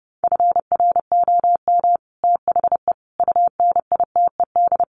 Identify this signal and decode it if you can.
Morse code, heard as a keyed tone, about 30 wpm, 700 Hz. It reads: FROM THE UNITED
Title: Sets of 4 Words 30wpm